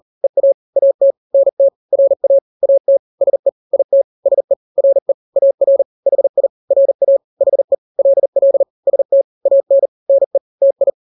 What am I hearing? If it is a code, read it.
Morse code, heard as a keyed tone, about 40 wpm, 550 Hz. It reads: EA AT NT RA AT SE IT SE RE AR HI RA HE LL ST AN NE TI
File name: n-grams / 2-grams 40 35 WPM